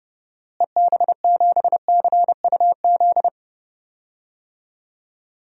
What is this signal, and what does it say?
Morse code, heard as a keyed tone, about 30 wpm, 700 Hz. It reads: EB7CUZ
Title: DXCC-100 Call Signs - 6 Characters - Repeated 2x - 30wpm